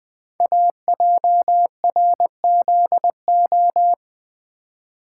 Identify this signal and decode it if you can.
Morse code, heard as a keyed tone, about 20 wpm, 700 Hz. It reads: AJRZO
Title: Code Groups - 5 Letter:Numbers - 20wpm